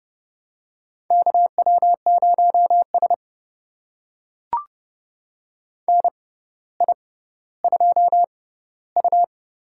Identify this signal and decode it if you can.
Morse code, heard as a keyed tone, about 30 wpm, spaced as for 15 wpm, 700 Hz. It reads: KW0S NI2U
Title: USA Call Signs - 4 Characters - 6x Character Spacing - 30wpm